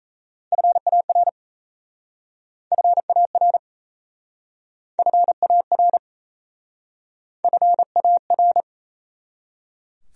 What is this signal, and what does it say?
Morse code, heard as a keyed tone, about 37 wpm, 700 Hz. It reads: FAR FAR FAR FAR